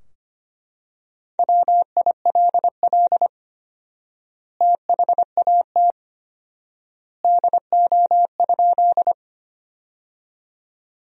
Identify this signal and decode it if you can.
Morse code, heard as a keyed tone, about 25 wpm, 700 Hz. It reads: WILL THAT DO?
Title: 4x Word-Spacing - Sentences from Top 100 Words - 25wpm